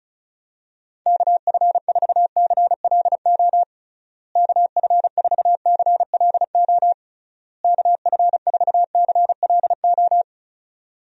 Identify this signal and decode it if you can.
Morse code, heard as a keyed tone, about 36 wpm, 700 Hz. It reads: KF4CLO KF4CLO KF4CLO
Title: USA Call Signs - 6 Characters - Repeated 3x - 35wpm